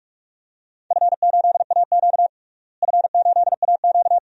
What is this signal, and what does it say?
Morse code, heard as a keyed tone, about 43 wpm, 700 Hz. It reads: F8AQ F8AQ
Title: DXCC-100 Call Signs - 4 Characters - Repeated 2x - 45wpm